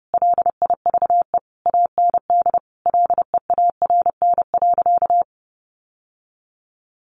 Morse code, heard as a keyed tone, about 30 wpm, 700 Hz. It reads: LIVE AND LEARN.
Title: Sentences from Top 800 Words 30wpm